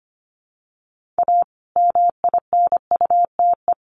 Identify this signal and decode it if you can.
Morse code, heard as a keyed tone, about 25 wpm, 700 Hz. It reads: A MINUTE
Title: Sentences from Top 1400 Words 25wpm